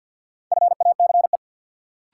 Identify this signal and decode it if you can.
Morse code, heard as a keyed tone, about 52 wpm, 700 Hz. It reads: FACE
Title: Sets of 3 Words - New Word Drill - 50wpm